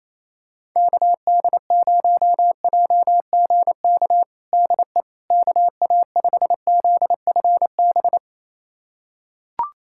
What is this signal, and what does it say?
Morse code, heard as a keyed tone, about 27 wpm, 700 Hz. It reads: KD0JGK DE KA5ZFB
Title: USA Callsign DE Callsign - 6 Characters - 28wpm